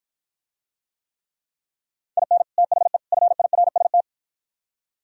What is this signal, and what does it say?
Morse code, heard as a keyed tone, about 53 wpm, 700 Hz. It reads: IN THE FIRST